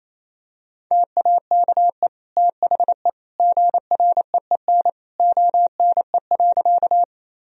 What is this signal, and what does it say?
Morse code, heard as a keyed tone, about 27 wpm, 700 Hz. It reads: TAKE THE GREEN ONE.
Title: Sentences from Top 1000 Words - Review Part 2 - 28wpm